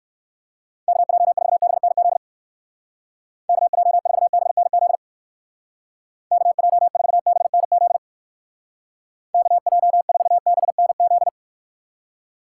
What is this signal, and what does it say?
Morse code, heard as a keyed tone, about 59 wpm, 700 Hz. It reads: KJ4BNZ KJ4BNZ KJ4BNZ KJ4BNZ